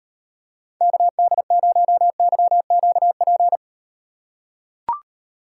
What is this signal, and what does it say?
Morse code, heard as a keyed tone, about 38 wpm, 700 Hz. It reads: KD0YQP